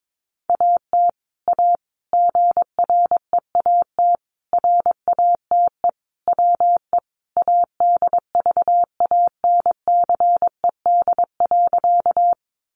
Morse code, heard as a keyed tone, about 22 wpm, 700 Hz. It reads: AT A GREAT RATE WE ADVANCED.
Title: Sentences from Top 1200 Words 22wpm